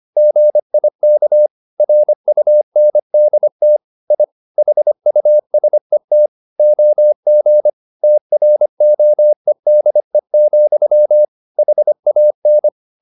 Morse code, heard as a keyed tone, about 25 wpm, 600 Hz. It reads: GIK RUNDT I HUSET OG TROEDE, HAN
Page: Ferie 25 wpm